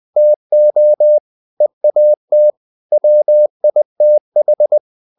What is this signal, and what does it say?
Morse code, heard as a keyed tone, about 20 wpm, 600 Hz. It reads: TO EAT WITH
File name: the old man and the sea / the old man and the sea part 1 20wpm